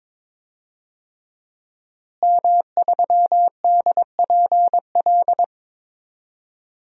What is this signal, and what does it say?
Morse code, heard as a keyed tone, about 22 wpm, 700 Hz. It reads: M3DPL
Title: DXCC-100 Call Signs - 5 Characters - Repeated 3x - 22wpm